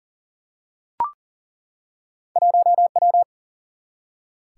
Morse code, heard as a keyed tone, about 40 wpm, 700 Hz. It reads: 1W
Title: Single-Letter Single-Number 40wpm